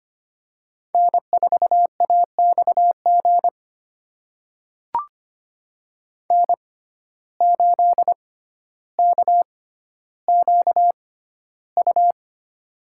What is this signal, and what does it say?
Morse code, heard as a keyed tone, about 25 wpm, spaced as for 13 wpm, 700 Hz. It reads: N4AXG N8KQU
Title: USA Call Signs - 5 Characters - 6x Character Spacing - 25wpm